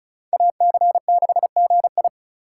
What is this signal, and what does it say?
Morse code, heard as a keyed tone, about 36 wpm, 700 Hz. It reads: AC6GI
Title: USA Call Signs - 4x Character Spacing - 35wpm